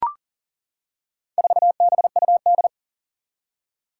Morse code, heard as a keyed tone, about 40 wpm, 700 Hz. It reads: 4BUD
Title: Four Challenging Letter-Number Combinations 40wpm